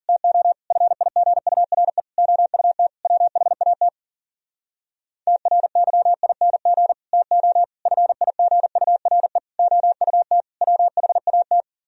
Morse code, heard as a keyed tone, about 52 wpm, 700 Hz. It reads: TO FIGURE OUT WHAT TRYING TO FIGURE OUT WHAT